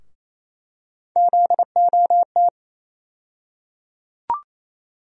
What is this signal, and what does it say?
Morse code, heard as a keyed tone, about 28 wpm, 700 Hz. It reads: ZOT
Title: USA Call Sign Postfixes - 3 Letters 28wpm